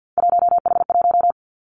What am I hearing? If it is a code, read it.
Morse code, heard as a keyed tone, about 50 wpm, 700 Hz. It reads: 1H1
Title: Three Challenging Letter-Number Combinations 50wpm